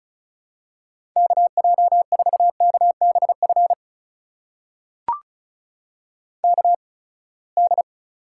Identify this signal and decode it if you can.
Morse code, heard as a keyed tone, about 36 wpm, spaced as for 24 wpm, 700 Hz. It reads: KJ4KBF KD
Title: USA Call Signs - 8x Character Spacing - 35wpm